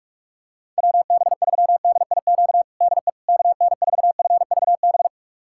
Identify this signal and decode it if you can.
Morse code, heard as a keyed tone, about 45 wpm, 700 Hz. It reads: WB3DIQ DE KN4FVB
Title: USA Callsign DE Callsign - 6 Characters - 45wpm